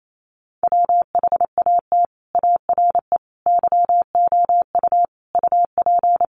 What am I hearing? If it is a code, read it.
Morse code, heard as a keyed tone, about 28 wpm, 700 Hz. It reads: WHAT ARE YOU UP
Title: Sentences from Top 100 Words 28wpm